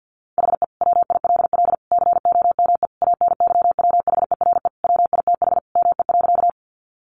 Morse code, heard as a keyed tone, about 48 wpm, 700 Hz. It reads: HE WILL COME ANYWHERE WITH ME.
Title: Sentences from Top 1400 Words 50wpm